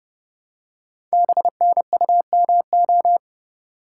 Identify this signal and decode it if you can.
Morse code, heard as a keyed tone, about 31 wpm, 690 Hz. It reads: BNUMO
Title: Code Groups - 5 Letters - 30wpm